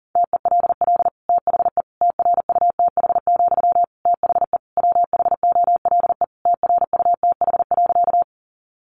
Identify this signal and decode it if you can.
Morse code, heard as a keyed tone, about 40 wpm, 700 Hz. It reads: TELL THE TRUTH, THE WHOLE TRUTH.
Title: Sentences from Top 800 Words 40wpm